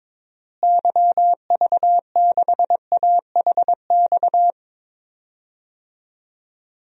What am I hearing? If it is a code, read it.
Morse code, heard as a keyed tone, about 22 wpm, 700 Hz. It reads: YV6AHX
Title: DXCC-100 Call Signs - 6 Characters - Repeated 3x - 22wpm